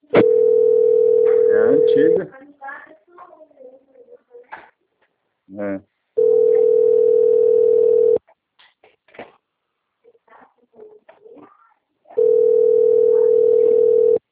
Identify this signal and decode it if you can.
Ringback tone (North America), 440 + 480 Hz, with speech.
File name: out- - - - - .2645